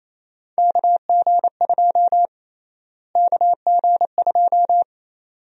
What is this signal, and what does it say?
Morse code, heard as a keyed tone, about 28 wpm, 700 Hz. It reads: KG2 KG2
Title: USA Call Sign Prefixes - Repeated 2x - 28wpm